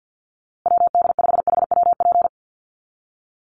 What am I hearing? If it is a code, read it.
Morse code, heard as a keyed tone, about 50 wpm, 700 Hz. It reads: WD5HWP
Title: USA Call Signs Repeated 2x 50wpm